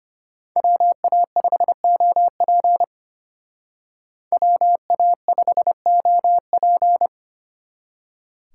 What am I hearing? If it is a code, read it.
Morse code, heard as a keyed tone, about 31 wpm, 700 Hz. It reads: WA5OP WA5OP
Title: Speed Racing - USA 2x2 Call Signs 25wpm